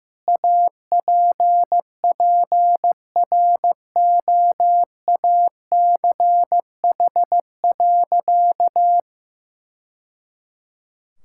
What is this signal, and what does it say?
Morse code, heard as a keyed tone, about 15 wpm, 700 Hz. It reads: APPROACH.
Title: Sentences from Top 1000 Words - Review Part 6 - 15wpm